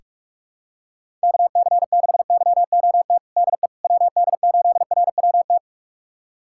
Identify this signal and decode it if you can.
Morse code, heard as a keyed tone, about 43 wpm, 700 Hz. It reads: KC6YOT DE WD8RWT
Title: USA Callsign DE Callsign 45wpm